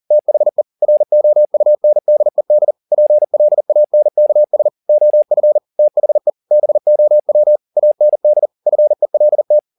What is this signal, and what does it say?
Morse code, heard as a keyed tone, about 40 wpm, 600 Hz. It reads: THE ROUNDED PLANKS OF THE BOW AND FELT